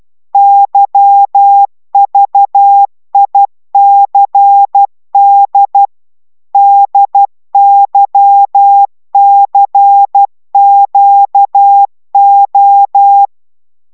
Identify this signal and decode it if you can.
Morse code, heard as a keyed tone, about 12 wpm, 800 Hz. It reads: YVICD DYCQO